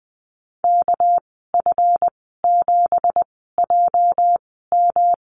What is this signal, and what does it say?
Morse code, heard as a keyed tone, about 20 wpm, spaced as for 14 wpm, 700 Hz. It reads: KF7JM
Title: 213 Lesson 34g - Number 8 - Call Signs 20wpm